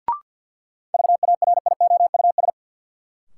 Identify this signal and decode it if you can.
Morse code, heard as a keyed tone, about 47 wpm, 700 Hz. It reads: VARIOUS
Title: Sentences from Top 1000 Words - Review Part 4 - New Word Drill - 50wpm